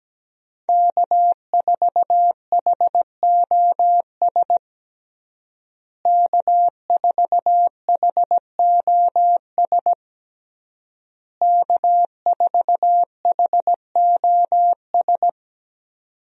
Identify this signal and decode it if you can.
Morse code, heard as a keyed tone, about 17 wpm, 700 Hz. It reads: K4HOS K4HOS K4HOS
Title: USA Call Signs - 5 Characters - Repeated 3x - 17wpm